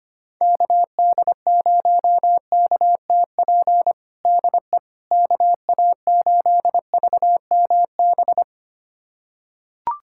Morse code, heard as a keyed tone, about 25 wpm, 700 Hz. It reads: KD0KTP DE KA8VMB
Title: USA Callsign DE Callsign 25wpm